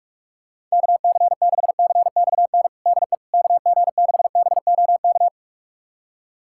Morse code, heard as a keyed tone, about 42 wpm, 700 Hz. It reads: KC6CXN DE KG6BOK
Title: USA Callsign DE Callsign - 6 Characters - 45wpm